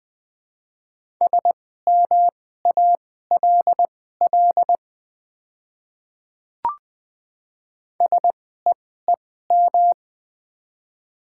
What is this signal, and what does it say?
Morse code, heard as a keyed tone, about 20 wpm, spaced as for 14 wpm, 700 Hz. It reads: SMALL SEEM
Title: 2x Character Spacing - Top 300 Words - 20wpm